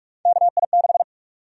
Morse code, heard as a keyed tone, about 46 wpm, 700 Hz. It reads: KI6
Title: USA Call Sign Prefixes - Repeated 2x - 45wpm